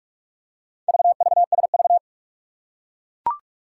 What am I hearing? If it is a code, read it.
Morse code, heard as a keyed tone, about 44 wpm, 700 Hz. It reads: VVSV
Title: Four Challenging Letter-Number Combinations 45wpm